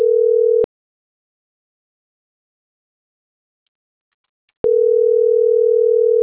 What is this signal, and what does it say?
Ringback tone (North America), 440 + 480 Hz.